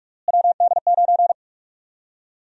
Morse code, heard as a keyed tone, about 43 wpm, 700 Hz. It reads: WD9
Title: USA Call Sign Prefixes - Repeated 2x - 45wpm